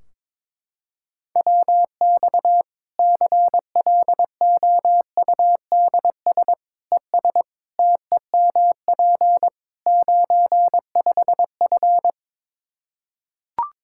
Morse code, heard as a keyed tone, about 22 wpm, 700 Hz. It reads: WX CLOUDS ES TEMP 95F
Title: Practice QSO - Exchange 2 - Info - 22wpm